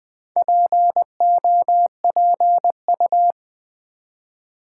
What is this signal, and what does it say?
Morse code, heard as a keyed tone, about 20 wpm, 700 Hz. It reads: POPU
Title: Build A Word - 4 Letters At A Time - 10-Letter Words - 20wpm